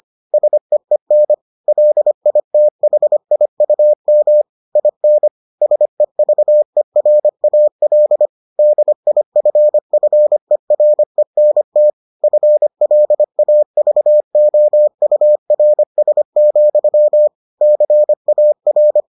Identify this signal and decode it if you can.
Morse code, heard as a keyed tone, about 25 wpm, 600 Hz. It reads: SEEN LITHIUM IN SEVERAL DIFFERENT FLAVOURS, CAR